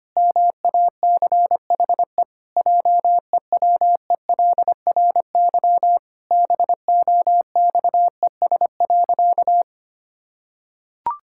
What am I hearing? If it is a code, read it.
Morse code, heard as a keyed tone, about 25 wpm, 700 Hz. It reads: MACHE JEWELRY BOXES.
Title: Pangram Challenge v2 - 25wpm